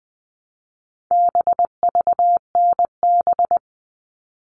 Morse code, heard as a keyed tone, about 20 wpm, 700 Hz. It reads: BVNB
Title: All Cut Numbers - 4-Digit 20wpm